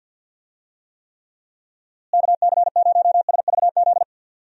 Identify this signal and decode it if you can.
Morse code, heard as a keyed tone, about 52 wpm, 700 Hz. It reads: KC0SVZ
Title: USA Call Signs - 6 Characters - Repeated 3x - 50wpm